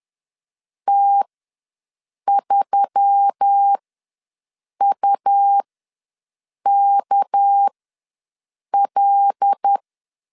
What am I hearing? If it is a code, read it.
Morse code, heard as a keyed tone, about 10 wpm, spaced as for 6 wpm, 800 Hz. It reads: T3UKL
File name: LEZIONI / 6 PKX0 / 6 PKX ws 10wpm 3706